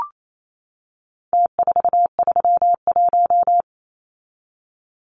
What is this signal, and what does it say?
Morse code, heard as a keyed tone, about 28 wpm, 700 Hz. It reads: T431
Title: Numbers - Contest Style - 4-Digit 28wpm